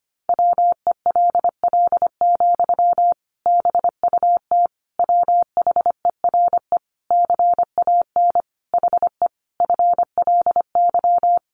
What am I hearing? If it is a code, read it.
Morse code, heard as a keyed tone, about 25 wpm, 700 Hz. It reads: WELL, BUT WHERE CAN HE FLY
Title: Sentences from Top 500 Words 25wpm